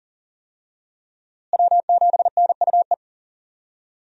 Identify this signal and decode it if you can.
Morse code, heard as a keyed tone, about 40 wpm, 700 Hz. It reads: W7NUE
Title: USA Call Signs - 5 Characters - Repeated 2x - 40wpm